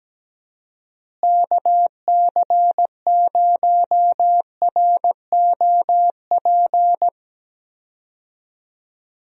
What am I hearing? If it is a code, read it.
Morse code, heard as a keyed tone, about 17 wpm, 700 Hz. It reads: KC0ROP
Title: USA Call Signs - 6 Characters - Repeated 2x - 17wpm